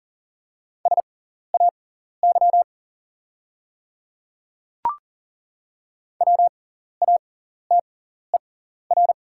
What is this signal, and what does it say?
Morse code, heard as a keyed tone, about 40 wpm, spaced as for 14 wpm, 690 Hz. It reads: SAY WATER
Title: 6x Character Spacing - Top 200 Words - 40wpm